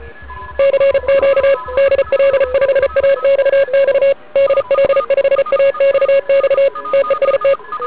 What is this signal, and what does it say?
Morse code, heard as a keyed tone, about 32 wpm, 560 Hz. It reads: CQ DL5AXX DL5AXX TEST
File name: ARCHIV / 40m cw1